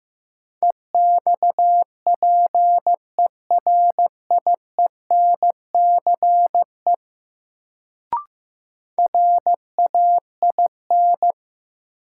Morse code, heard as a keyed tone, about 15 wpm, 700 Hz. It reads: EXPERIENCE RAIN
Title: Sentences from Top 600 Words - New Word Drill - 15wpm